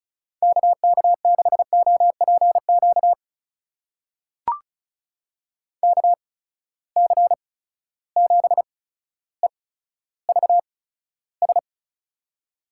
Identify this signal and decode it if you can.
Morse code, heard as a keyed tone, about 36 wpm, spaced as for 15 wpm, 700 Hz. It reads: KK6OPQ KC7EVS
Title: USA Call Signs - 6 Characters - 8x Character Spacing - 35wpm